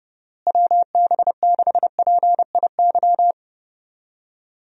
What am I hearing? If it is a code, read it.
Morse code, heard as a keyed tone, about 32 wpm, 700 Hz. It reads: WB6PIY
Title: USA Call Signs - 6 Characters - Repeated 2x - 30wpm